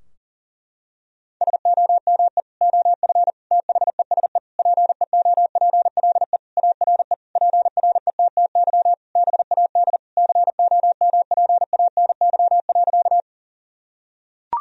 Morse code, heard as a keyed tone, about 37 wpm, 700 Hz. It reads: SOME OF THESE PEOPLE ARE PRETTY BAD COMPANY.
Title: Sentences from Top 1000 Words - Review Part 3 - 40wpm